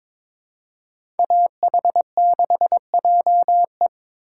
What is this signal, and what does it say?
Morse code, heard as a keyed tone, about 22 wpm, 700 Hz. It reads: AH6JE
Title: USA Call Signs - 5 Characters - 3x Character Spacing - 22wpm